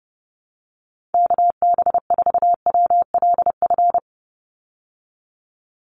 Morse code, heard as a keyed tone, about 30 wpm, 700 Hz. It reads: KB4WLF
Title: Call Signs Repeated 3x - 30wpm